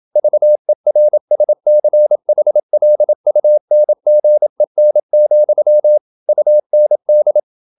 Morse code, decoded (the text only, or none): VERSCHLUNGEN, UND